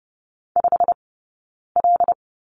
Morse code, heard as a keyed tone, about 30 wpm, spaced as for 9 wpm, 690 Hz. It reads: 5L
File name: Lesson 18d - Letter m - Two Characters 30wpm